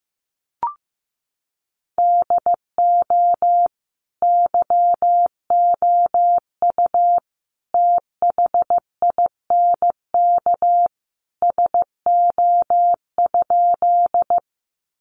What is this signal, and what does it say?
Morse code, heard as a keyed tone, about 15 wpm, 700 Hz. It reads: DO YOU THINK SO?
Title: Sentences from Top 200 Words 15wpm